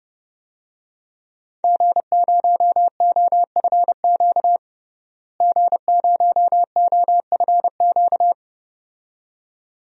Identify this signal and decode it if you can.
Morse code, heard as a keyed tone, about 32 wpm, 700 Hz. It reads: G0OFQ G0OFQ
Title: DXCC-100 Call Signs - 5 Characters - Repeated 2x - 30wpm